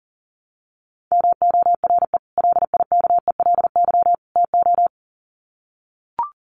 Morse code, heard as a keyed tone, about 40 wpm, 700 Hz. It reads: MORE LIKELY TO
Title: Sets of 3 Words 40wpm